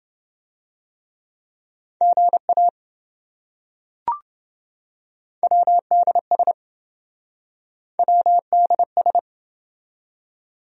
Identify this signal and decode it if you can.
Morse code, heard as a keyed tone, about 28 wpm, 700 Hz. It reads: GA WDS WDS